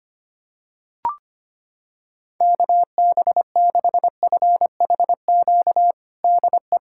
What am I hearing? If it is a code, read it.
Morse code, heard as a keyed tone, about 25 wpm, 700 Hz. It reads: KB6FHQ DE
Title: USA Callsign DE Callsign 25wpm